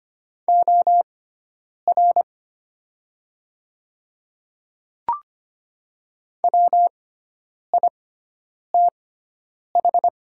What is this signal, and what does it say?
Morse code, heard as a keyed tone, about 25 wpm, spaced as for 9 wpm, 700 Hz. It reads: OR WITH
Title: 6x Character Spacing - Top 100 Words - 25wpm